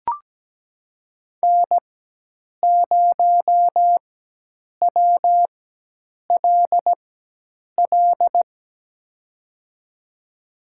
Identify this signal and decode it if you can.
Morse code, heard as a keyed tone, about 17 wpm, spaced as for 8 wpm, 700 Hz. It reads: N0WLL
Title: USA Call Signs - 5 Characters - 4x Character Spacing - 17wpm